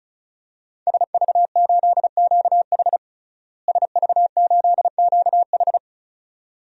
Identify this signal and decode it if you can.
Morse code, heard as a keyed tone, about 34 wpm, 700 Hz. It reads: SV8QH SV8QH
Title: DXCC-100 Call Signs - 5 Characters - Repeated 2x - 35wpm